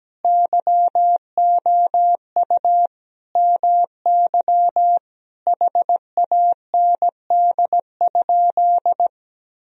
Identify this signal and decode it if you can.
Morse code, heard as a keyed tone, about 17 wpm, 700 Hz. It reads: YOU MY HAND?
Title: Sentences from Top 1000 Words - Review Part 4 - 17wpm